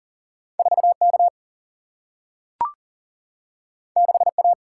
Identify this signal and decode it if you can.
Morse code, heard as a keyed tone, about 39 wpm, 700 Hz. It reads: 4K 6A
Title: Single-Letter Single-Number 40wpm